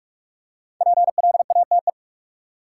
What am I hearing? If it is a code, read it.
Morse code, heard as a keyed tone, about 46 wpm, 700 Hz. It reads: PLATE
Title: Five Letter Words - 45wpm